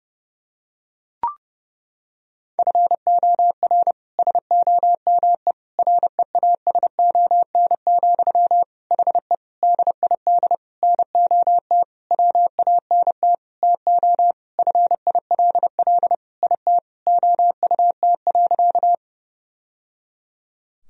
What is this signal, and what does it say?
Morse code, heard as a keyed tone, about 30 wpm, 700 Hz. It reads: FOR SOME REASON, HE DID NOT WANT TO FILL IT OUT.
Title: Sentences from Top 1000 Words - Review Part 3 - 30wpm